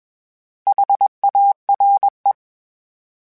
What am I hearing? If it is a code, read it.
Morse code, heard as a keyed tone, about 21 wpm, 800 Hz. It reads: HARE